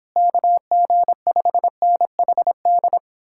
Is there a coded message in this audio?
Yes: Morse code from 0.2 to 3.0 s, heard as a keyed tone, about 31 wpm, 700 Hz. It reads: KG5NHD